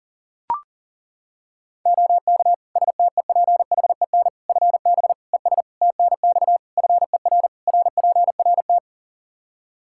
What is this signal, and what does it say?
Morse code, heard as a keyed tone, about 40 wpm, 700 Hz. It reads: OK STEPHEN FB ES TNX FER RPRT
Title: K1USN SST QSO Practice 40wpm